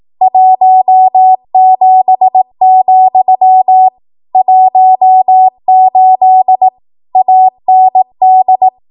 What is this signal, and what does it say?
Morse code, heard as a keyed tone, about 18 wpm, 750 Hz. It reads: 17, 18 AND
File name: Archive / 18 WPM / 18WPM